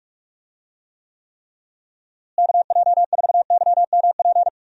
Morse code, heard as a keyed tone, about 46 wpm, 700 Hz. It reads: KJ4YMP
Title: USA Call Signs - 6 Characters - Repeated 3x - 45wpm